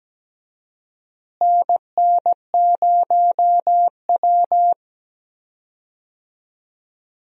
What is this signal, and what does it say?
Morse code, heard as a keyed tone, about 17 wpm, 700 Hz. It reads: NN0W
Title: USA Call Signs - 4 Characters - Repeated 3x - 17wpm